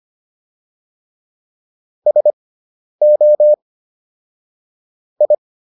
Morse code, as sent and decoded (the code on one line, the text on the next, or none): ... --- / ..
SO I